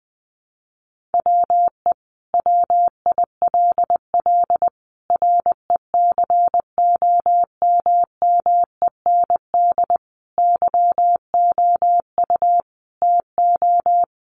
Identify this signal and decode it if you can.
Morse code, heard as a keyed tone, about 19 wpm, 700 Hz. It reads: WE WILL RECOMMEND YOU TO
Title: Sentences from Top 1200 Words 20wpm